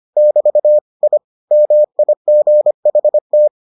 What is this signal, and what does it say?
Morse code, heard as a keyed tone, about 25 wpm, 600 Hz. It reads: <BT> I MIGHT